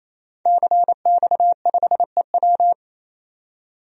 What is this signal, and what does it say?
Morse code, heard as a keyed tone, about 28 wpm, 700 Hz. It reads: CX5EW
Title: DXCC-100 Call Signs - 5 Characters - 28wpm